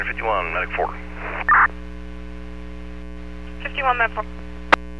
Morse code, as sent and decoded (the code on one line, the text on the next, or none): none